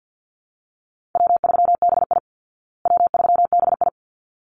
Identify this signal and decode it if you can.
Morse code, heard as a keyed tone, about 50 wpm, 700 Hz. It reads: W3BI W3BI
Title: Easy Call Signs Repeated 2x - 50wpm